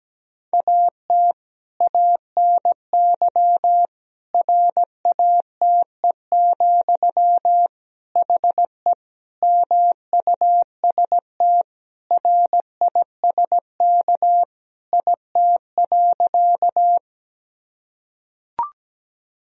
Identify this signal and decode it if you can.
Morse code, heard as a keyed tone, about 17 wpm, 700 Hz. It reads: AT ANY RATE, HE MUST RISK IT.
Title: Sentences from Top 1000 Words - Review Part 4 - 17wpm